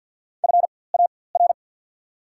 Morse code, heard as a keyed tone, about 48 wpm, spaced as for 23 wpm, 690 Hz. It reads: FAR
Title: 4x Character Spacing - Top 300 Words - 50wpm